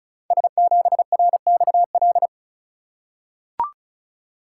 Morse code, heard as a keyed tone, about 36 wpm, 700 Hz. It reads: S7RXL
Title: Code Groups - 5 Letter:Numbers - 35wpm